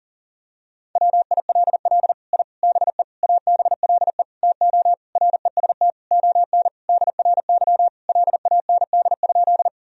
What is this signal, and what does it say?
Morse code, heard as a keyed tone, about 40 wpm, 700 Hz. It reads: WILL I BE ABLE TO REST ON DRY LAND?
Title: Sentences from Top 1000 Words - Review Part 3 - 40wpm